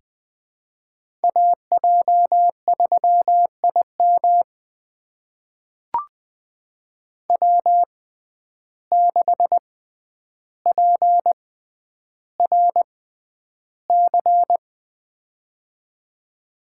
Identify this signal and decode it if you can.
Morse code, heard as a keyed tone, about 20 wpm, spaced as for 10 wpm, 700 Hz. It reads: AJ3IM W6PRC
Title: USA Call Signs - 5 Characters - 6x Character Spacing - 20wpm